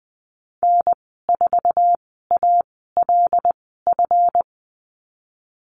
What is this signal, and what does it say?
Morse code, heard as a keyed tone, about 20 wpm, spaced as for 15 wpm, 700 Hz. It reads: N4ALF
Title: 173 Lesson 28g - Number 9 - Call Signs 20wpm